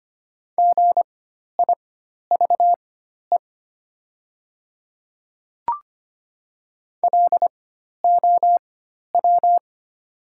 Morse code, heard as a keyed tone, about 25 wpm, spaced as for 12 wpm, 700 Hz. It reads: GIVE LOW
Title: 4x Character Spacing - Top 200 Words - 25wpm